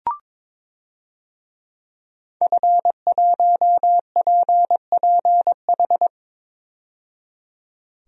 Morse code, heard as a keyed tone, about 22 wpm, 700 Hz. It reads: F1PPH
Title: DXCC-100 Call Signs - 5 Characters - 22wpm